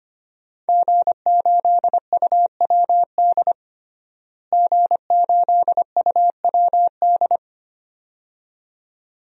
Morse code, heard as a keyed tone, about 25 wpm, 700 Hz. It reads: G8UWD G8UWD
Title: DXCC-100 Call Signs - 5 Characters - Repeated 2x - 25wpm